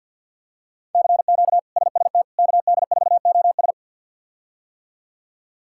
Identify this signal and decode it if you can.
Morse code, heard as a keyed tone, about 48 wpm, 700 Hz. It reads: CQ SST KD4OS
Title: K1USN SST QSO Practice - 50wpm